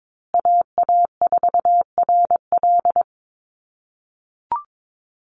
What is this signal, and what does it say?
Morse code, heard as a keyed tone, about 22 wpm, 700 Hz. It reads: AA4RL
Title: USA Call Signs Repeated 2x 22wpm